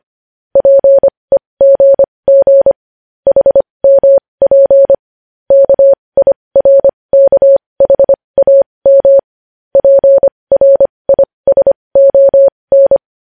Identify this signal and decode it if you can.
Morse code, heard as a keyed tone, about 25 wpm, spaced as for 20 wpm, 550 Hz. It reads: PEGG HMP KIRKHAM PRISON
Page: news headlines in morse code at 20 wpm